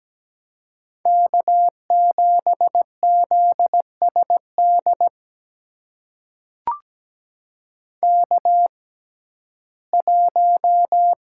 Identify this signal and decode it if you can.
Morse code, heard as a keyed tone, about 17 wpm, spaced as for 12 wpm, 700 Hz. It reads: K7ZSD K1
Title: USA Call Signs - 5 Characters - 6x Character Spacing - 17wpm